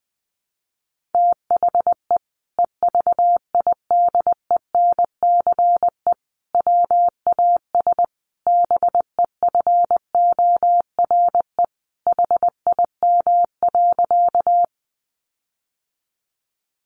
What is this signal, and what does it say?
Morse code, heard as a keyed tone, about 20 wpm, 700 Hz. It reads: THE EVIDENCE WAS BEFORE HIM.
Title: Sentences from Top 1300 Words 20wpm